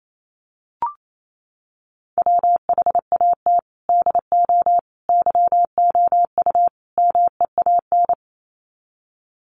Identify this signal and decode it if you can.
Morse code, heard as a keyed tone, about 28 wpm, 700 Hz. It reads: WHAT DO YOU MEAN
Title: Sets of 4 Words 28wpm